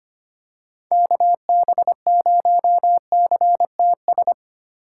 Morse code, heard as a keyed tone, about 25 wpm, 700 Hz. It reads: KB0CTS
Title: USA Call Signs - 6 Characters - 2x Character Spacing - 25wpm